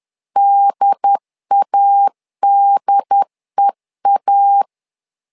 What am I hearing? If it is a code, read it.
Morse code, heard as a keyed tone, about 10 wpm, 800 Hz. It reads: DADEA
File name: LEZIONI / 2 UHDN / 2 UHD 10wpm 1826